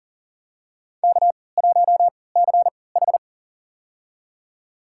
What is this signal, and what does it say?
Morse code, heard as a keyed tone, about 40 wpm, spaced as for 23 wpm, 700 Hz. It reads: K1CH
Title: USA Call Signs - 4 Characters - 3x Character Spacing - 40wpm